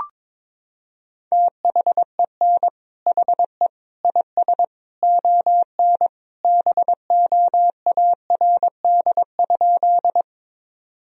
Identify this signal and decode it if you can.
Morse code, heard as a keyed tone, about 22 wpm, 700 Hz. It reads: THEN HE IS ON BOARD?
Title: Sentences from Top 1000 Words - Review Part 4 - 22wpm